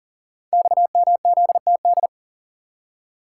Morse code, heard as a keyed tone, about 40 wpm, 700 Hz. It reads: XMZTD
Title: Code Groups - 5 Letter:Numbers - 40wpm